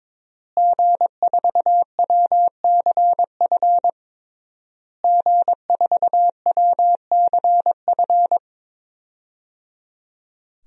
Morse code, heard as a keyed tone, about 22 wpm, 700 Hz. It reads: G4WCF G4WCF
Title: DXCC-100 Call Signs - 5 Characters - Repeated 3x - 22wpm